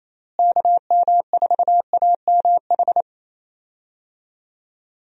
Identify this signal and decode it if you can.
Morse code, heard as a keyed tone, about 28 wpm, 700 Hz. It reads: KM4AMH
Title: USA Call Signs - 6 Characters - Repeated 3x - 28wpm